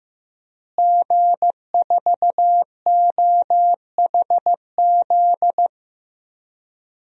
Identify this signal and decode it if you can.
Morse code, heard as a keyed tone, about 15 wpm, 700 Hz. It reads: G4OHZ
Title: DXCC-100 Call Signs - 5 Characters - Repeated 3x - 15wpm